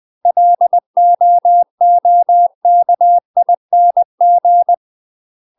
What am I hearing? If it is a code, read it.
Morse code, heard as a keyed tone, about 20 wpm, 700 Hz. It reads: LOOKING
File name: UK MASTER QSOs TRAINING / 20WPM / UK MASTER QSOs 0031